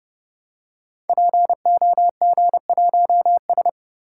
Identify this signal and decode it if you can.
Morse code, heard as a keyed tone, about 30 wpm, 700 Hz. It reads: POG1S
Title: Code Groups - 5 Letter:Numbers - 30wpm